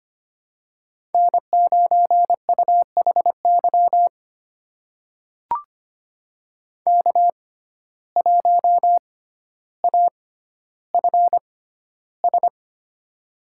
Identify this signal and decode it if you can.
Morse code, heard as a keyed tone, about 25 wpm, spaced as for 13 wpm, 700 Hz. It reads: N9UHY K1AFS
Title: USA Call Signs - 5 Characters - 6x Character Spacing - 25wpm